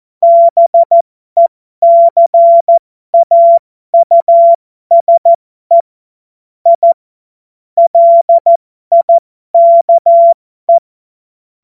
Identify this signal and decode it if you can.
Morse code, heard as a keyed tone, about 14 wpm, spaced as for 12 wpm, 680 Hz. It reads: BECAUSE I LIKE
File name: Morsekurs / Witze